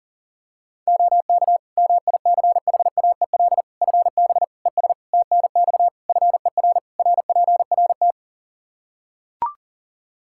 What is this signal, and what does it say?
Morse code, heard as a keyed tone, about 40 wpm, 700 Hz. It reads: OK MICHAEL FB ES TNX FER RPRT
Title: K1USN SST QSO Practice 40wpm